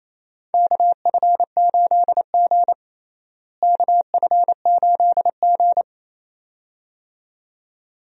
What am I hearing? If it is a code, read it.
Morse code, heard as a keyed tone, about 28 wpm, 700 Hz. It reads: KF8G KF8G
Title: USA Call Signs - 4 Characters - Repeated 2x - 28wpm